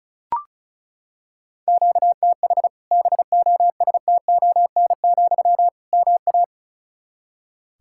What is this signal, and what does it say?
Morse code, heard as a keyed tone, about 36 wpm, 700 Hz. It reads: QTH BOSTON, MA
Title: Practice QSO - Exchange 1 - Beginning - 35wpm